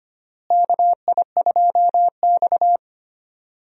Morse code, heard as a keyed tone, about 25 wpm, 700 Hz. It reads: KI2X
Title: USA Call Signs - 4 Characters - Repeated 3x - 25wpm